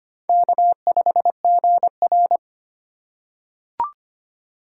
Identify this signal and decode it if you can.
Morse code, heard as a keyed tone, about 25 wpm, 700 Hz. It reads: K5GR
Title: USA Call Signs - 3x Character Spacing - 25wpm